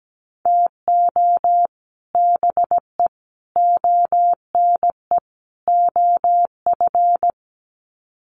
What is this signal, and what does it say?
Morse code, heard as a keyed tone, about 17 wpm, 700 Hz. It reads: TO BE ONE OF
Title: Sets of 4 Words 17wpm